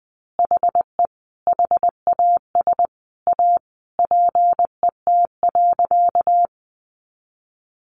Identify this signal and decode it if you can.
Morse code, heard as a keyed tone, about 20 wpm, 700 Hz. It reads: HE HAS A PET.
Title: Easy Phrases 20wpm